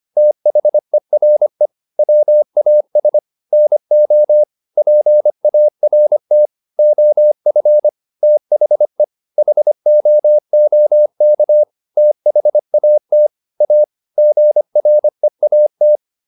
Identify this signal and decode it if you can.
Morse code, heard as a keyed tone, about 25 wpm, 600 Hz. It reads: THERE WAS NO PART OF THE HOOK THAT A GREAT
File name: the old man and the sea part 1 25wpm